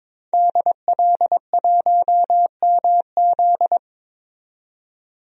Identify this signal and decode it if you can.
Morse code, heard as a keyed tone, about 22 wpm, 700 Hz. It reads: DL1MZ
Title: DXCC-100 Call Signs - 5 Characters - 22wpm